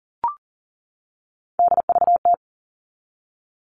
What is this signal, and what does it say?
Morse code, heard as a keyed tone, about 40 wpm, 700 Hz. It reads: DVT
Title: All Cut Numbers - 3-Digit 40wpm